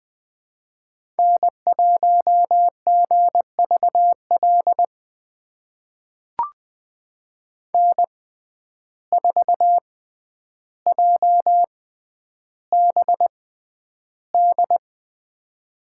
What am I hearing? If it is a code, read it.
Morse code, heard as a keyed tone, about 20 wpm, spaced as for 10 wpm, 700 Hz. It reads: N1GVL N4JBD
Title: USA Call Signs - 5 Characters - 6x Character Spacing - 20wpm